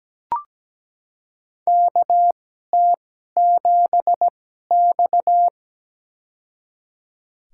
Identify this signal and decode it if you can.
Morse code, heard as a keyed tone, about 17 wpm, spaced as for 12 wpm, 700 Hz. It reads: KT7X
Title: USA Call Signs - 4 Characters - 2x Character Spacing - 17wpm